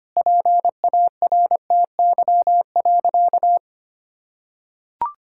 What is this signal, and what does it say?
Morse code, heard as a keyed tone, about 25 wpm, 700 Hz. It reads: PARTY.
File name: practice / 25wpm / Sentences with 3-Word Phrases - 25wpm